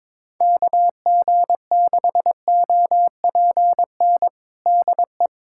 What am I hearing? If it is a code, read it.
Morse code, heard as a keyed tone, about 21 wpm, 700 Hz. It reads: KG6OPN DE
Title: USA Callsign DE Callsign 22wpm